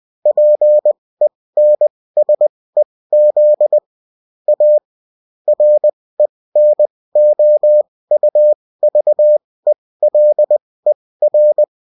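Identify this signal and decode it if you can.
Morse code, heard as a keyed tone, about 20 wpm, spaced as for 16 wpm, 600 Hz. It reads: PENSEZ A RENOUVELER
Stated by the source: Morse code bulletin of a French amateur CW union, sent as text and in CW.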